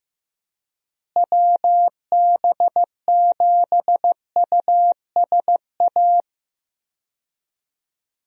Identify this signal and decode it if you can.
Morse code, heard as a keyed tone, about 15 wpm, 700 Hz. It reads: WB7USA
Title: USA Call Signs - 6 Characters - Repeated 3x - 15wpm